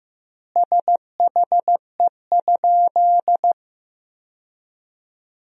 Morse code, heard as a keyed tone, about 15 wpm, 700 Hz. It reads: SHE?
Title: 2x Word-Spacing - Sentences from Top 200 Words - 15wpm